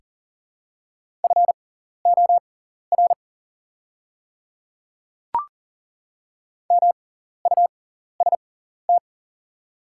Morse code, heard as a keyed tone, about 40 wpm, spaced as for 14 wpm, 700 Hz. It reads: FOR MUST
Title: 6x Character Spacing - Top 300 Words - 40wpm